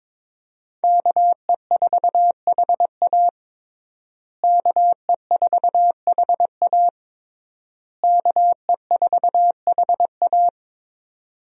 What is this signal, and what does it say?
Morse code, heard as a keyed tone, about 22 wpm, 700 Hz. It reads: KE4HA KE4HA KE4HA
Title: USA Call Signs - 5 Characters - Repeated 3x - 22wpm